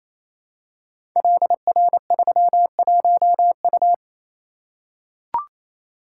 Morse code, heard as a keyed tone, about 29 wpm, 700 Hz. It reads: LR31U
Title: Code Groups - 5 Letter:Numbers - 28wpm